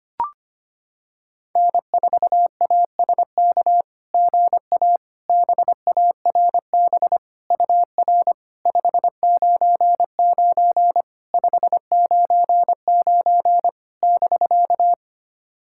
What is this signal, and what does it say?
Morse code, heard as a keyed tone, about 25 wpm, 700 Hz. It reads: N4ASK GA BARB UR 599 599 <BK>
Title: SOTA QSO Practice 25wpm